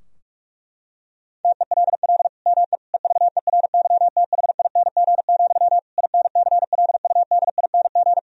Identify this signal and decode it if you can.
Morse code, heard as a keyed tone, about 44 wpm, 700 Hz. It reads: TELL ME EVERYTHING, INCLUDING
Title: Sentences from Top 1000 Words - Review Part 4 - 45wpm